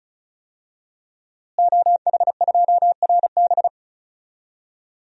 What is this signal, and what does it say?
Morse code, heard as a keyed tone, about 36 wpm, 700 Hz. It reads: OH2RB
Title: DXCC-100 Call Signs - 5 Characters - Repeated 2x - 35wpm